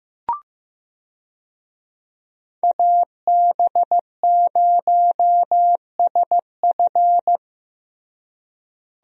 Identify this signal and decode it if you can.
Morse code, heard as a keyed tone, about 16 wpm, 700 Hz. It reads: AB0SF
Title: USA Call Signs - 5 Characters - Repeated 3x - 15wpm